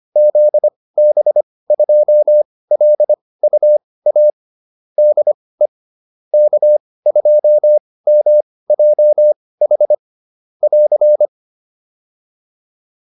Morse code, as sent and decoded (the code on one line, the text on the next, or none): --.. -... ..--- .-.. ..- .- / -.. / -.- ..--- -- .--- .... / .-.-.
ZB2LUA DE K2MJH <AR>